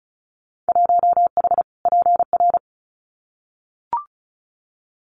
Morse code, heard as a keyed tone, about 36 wpm, 700 Hz. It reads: 1H PR
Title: Winter Field Day Exchanges Encore 35wpm